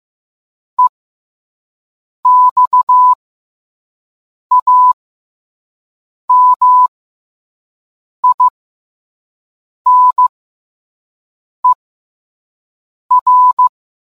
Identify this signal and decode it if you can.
Morse code, heard as a keyed tone, about 15 wpm, spaced as for 5 wpm, 1000 Hz. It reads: EXAMINER